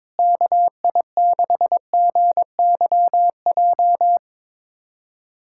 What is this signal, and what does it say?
Morse code, heard as a keyed tone, about 22 wpm, 700 Hz. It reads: KI6GYJ
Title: USA Callsign DE Callsign 22wpm